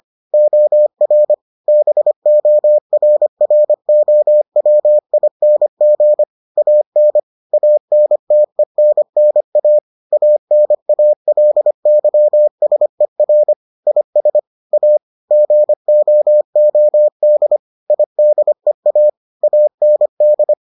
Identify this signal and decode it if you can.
Morse code, heard as a keyed tone, about 25 wpm, 600 Hz. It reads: OR BORROWING AN ANTENNA ANALYSER IS A GOOD IDEA AND